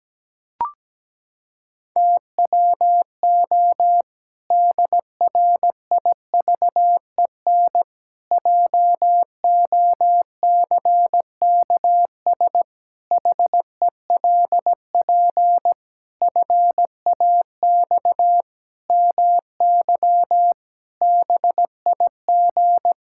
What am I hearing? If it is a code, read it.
Morse code, heard as a keyed tone, about 17 wpm, 700 Hz. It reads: TWO DRIVEN JOCKS HELP FAX MY BIG
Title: Pangram Challenge v2 - 17wpm